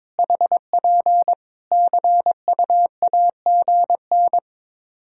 Morse code, heard as a keyed tone, about 22 wpm, 700 Hz. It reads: HP CUAGN
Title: Practice QSO - Exchange 4 - Ending - 22wpm